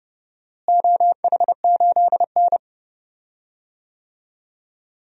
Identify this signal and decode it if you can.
Morse code, heard as a keyed tone, about 32 wpm, 700 Hz. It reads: OH8N
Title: DXCC-100 Call Signs - 4 Characters - Repeated 3x - 30wpm